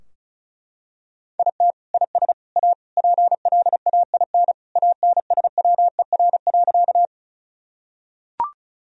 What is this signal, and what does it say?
Morse code, heard as a keyed tone, about 36 wpm, 700 Hz. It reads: IT IS A PLAIN ANSWER.
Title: Sentences from Top 1000 Words - Review Part 2 - 35wpm